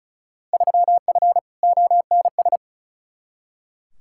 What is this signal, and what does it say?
Morse code, heard as a keyed tone, about 36 wpm, 700 Hz. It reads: 3F ONS
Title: Field Day Exchanges - 35wpm